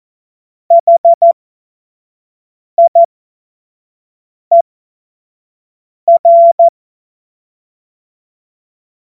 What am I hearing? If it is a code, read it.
Morse code, heard as a keyed tone, about 14 wpm, spaced as for 5 wpm, 680 Hz. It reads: HIER
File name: Morsekurs / Witze / Witz 10. August 2021, 25 F70